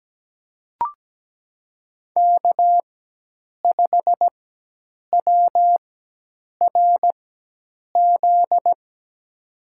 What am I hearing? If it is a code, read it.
Morse code, heard as a keyed tone, about 17 wpm, spaced as for 8 wpm, 700 Hz. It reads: K5WRZ
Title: USA Call Signs - 5 Characters - 4x Character Spacing - 17wpm